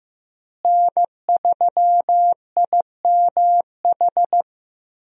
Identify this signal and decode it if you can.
Morse code, heard as a keyed tone, about 15 wpm, 700 Hz. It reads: N3IMH
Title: USA Call Signs - 5 Characters - Repeated 2x - 15wpm